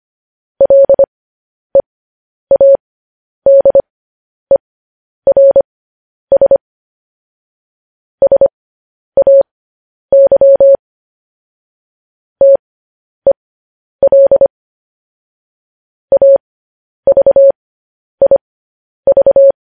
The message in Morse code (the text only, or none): LEADERS SAY TEL AVIV